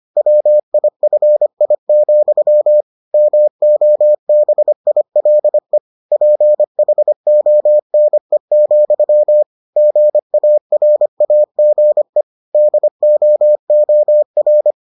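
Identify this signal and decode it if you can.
Morse code, heard as a keyed tone, about 25 wpm, 600 Hz. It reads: WIFI, MOBILE PHONE, GARAGE DOOR
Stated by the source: Morse edition of the Foundations of Amateur Radio episodes.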